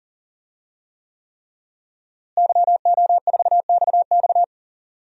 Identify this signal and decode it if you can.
Morse code, heard as a keyed tone, about 40 wpm, 700 Hz. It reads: YO4XX
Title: DXCC-100 Call Signs - 5 Characters - Repeated 3x - 40wpm